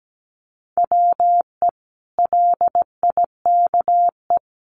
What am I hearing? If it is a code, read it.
Morse code, heard as a keyed tone, about 17 wpm, 700 Hz. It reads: WE LIKE
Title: Sentences from Top 1100 Words 17wpm